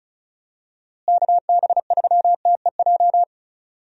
Morse code, heard as a keyed tone, about 36 wpm, 700 Hz. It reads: KB3TEJ
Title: USA Call Signs - 6 Characters - Repeated 2x - 35wpm